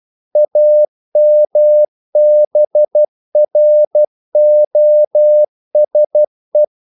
Morse code, heard as a keyed tone, about 12 wpm, 600 Hz. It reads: AMBROSE
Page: English quotations in CW (telegraphy - morse)